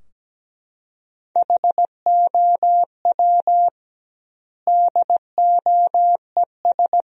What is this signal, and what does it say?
Morse code, heard as a keyed tone, about 17 wpm, 700 Hz. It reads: HOW DOES
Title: 2x Word-Spacing - Sentences from Top 400 Words - 17wpm